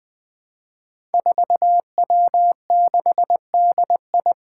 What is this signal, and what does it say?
Morse code, heard as a keyed tone, about 20 wpm, 700 Hz. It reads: 4W6DI
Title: Five Challenging Letter-Number Combinations 20wpm